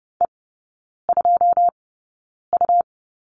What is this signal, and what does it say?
Morse code, heard as a keyed tone, about 30 wpm, spaced as for 9 wpm, 700 Hz. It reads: E2U
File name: Lesson 17e - Letter u - Three Characters 30wpm